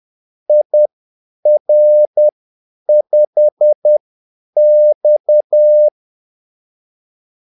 Morse code, heard as a keyed tone, about 10 wpm, spaced as for 8 wpm, 600 Hz. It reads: IR5X
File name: Callsigns-10-8wpm